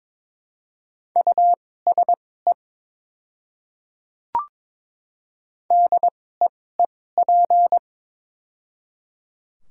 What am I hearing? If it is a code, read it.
Morse code, heard as a keyed tone, about 22 wpm, spaced as for 16 wpm, 700 Hz. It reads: USE DEEP
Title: 2x Character Spacing - Top 500 Words - 22wpm